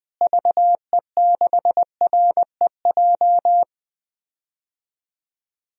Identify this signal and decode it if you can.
Morse code, heard as a keyed tone, about 20 wpm, 700 Hz. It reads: VE6REJ
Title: DXCC-100 Call Signs - 6 Characters - Repeated 3x - 20wpm